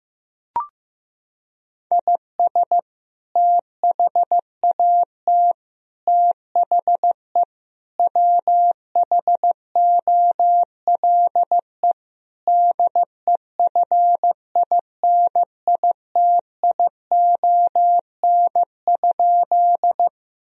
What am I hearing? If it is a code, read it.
Morse code, heard as a keyed tone, about 15 wpm, 700 Hz. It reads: IS THAT THE WHOLE DEFINITION?
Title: Sentences from Top 1000 Words - Review Part 4 - 15wpm